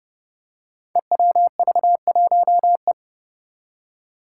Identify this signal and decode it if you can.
Morse code, heard as a keyed tone, about 30 wpm, 700 Hz. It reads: EWV1E
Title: Five Challenging Letter-Number Combinations 30wpm